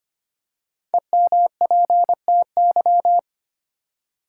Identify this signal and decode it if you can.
Morse code, heard as a keyed tone, about 25 wpm, 700 Hz. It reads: EMPTY
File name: 5 Letter Words - First 3 Letters Given - 25wpm